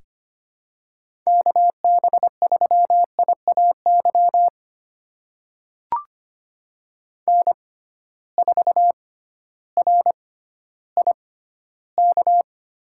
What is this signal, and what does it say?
Morse code, heard as a keyed tone, about 25 wpm, spaced as for 14 wpm, 700 Hz. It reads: KB3IAY N4RIK
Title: USA Call Signs - 6x Character Spacing - 25wpm